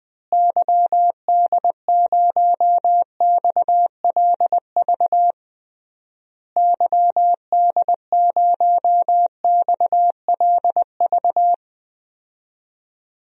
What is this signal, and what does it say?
Morse code, heard as a keyed tone, about 20 wpm, 700 Hz. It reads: YD0XLV YD0XLV
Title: DXCC-100 Call Signs - 6 Characters - Repeated 3x - 20wpm